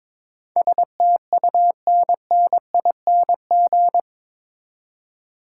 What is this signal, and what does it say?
Morse code, heard as a keyed tone, about 21 wpm, 700 Hz. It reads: STUNNING
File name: practice / 22wpm / Eight Letter Words - 22wpm